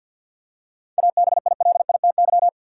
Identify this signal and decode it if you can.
Morse code, heard as a keyed tone, about 50 wpm, 700 Hz. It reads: ABILITY
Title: Seven Letter Words - 50wpm